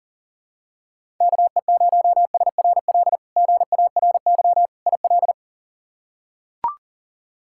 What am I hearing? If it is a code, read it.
Morse code, heard as a keyed tone, about 40 wpm, 700 Hz. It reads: KE0SRL GARY IL
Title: K1USN SST QSO Practice - 40wpm